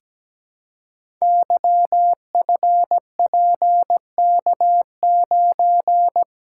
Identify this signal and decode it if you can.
Morse code, heard as a keyed tone, about 17 wpm, 700 Hz. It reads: YFPK9
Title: Code Groups - 5 Letter:Numbers - 17wpm